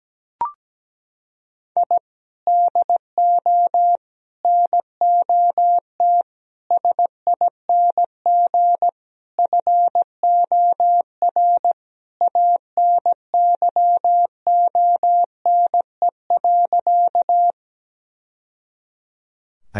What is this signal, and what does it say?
Morse code, heard as a keyed tone, about 17 wpm, 700 Hz. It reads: I DO NOT SING FOR ANYONE.
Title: Sentences from Top 1000 Words - Review Part 2 - 17wpm